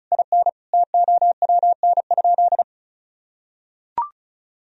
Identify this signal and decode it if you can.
Morse code, heard as a keyed tone, about 34 wpm, 700 Hz. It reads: IN TOWN?
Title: Sentences from Top 1000 Words - Review Part 4 - 35wpm